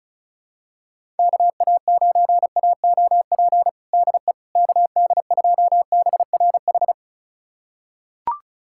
Morse code, heard as a keyed tone, about 34 wpm, 700 Hz. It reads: KA9AOP DE KD2BRH
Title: USA Callsign DE Callsign - 6 Characters - 35wpm